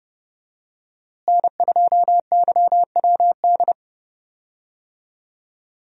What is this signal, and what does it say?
Morse code, heard as a keyed tone, about 32 wpm, 700 Hz. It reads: N2YWD
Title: USA Call Signs - 5 Characters - Repeated 3x - 30wpm